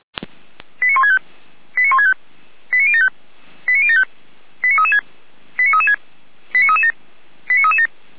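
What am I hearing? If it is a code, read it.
Five-tone selective call (ZVEI), heard as quick runs of five tones, tones of about 70 ms. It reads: 89165 89165 89075 89075 89336 89336 89337 89337
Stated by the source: ZVEI five-tone call received on an SDR.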